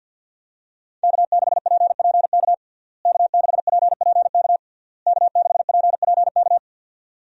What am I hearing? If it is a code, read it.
Morse code, heard as a keyed tone, about 47 wpm, 700 Hz. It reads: K6PPK K6PPK K6PPK
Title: USA Call Signs - 5 Characters - Repeated 3x - 50wpm